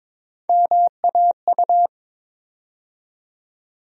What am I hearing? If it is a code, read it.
Morse code, heard as a keyed tone, about 22 wpm, 700 Hz. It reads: MAU
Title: USA Call Sign Postfixes - 3 Letters 22wpm